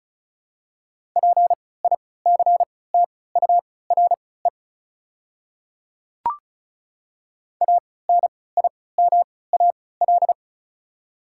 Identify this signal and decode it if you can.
Morse code, heard as a keyed tone, about 36 wpm, spaced as for 20 wpm, 700 Hz. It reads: PICTURE ANIMAL
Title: 3x Character Spacing - Top 200 Words - 35wpm